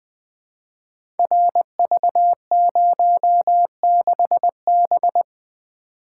Morse code, heard as a keyed tone, about 20 wpm, 700 Hz. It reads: RV06B
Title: Code Groups - 5 Letter:Numbers - 20wpm